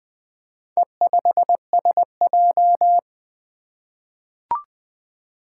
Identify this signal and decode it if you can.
Morse code, heard as a keyed tone, about 20 wpm, 700 Hz. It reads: E5SJ
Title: Four Letter-Number Combinations - 20wpm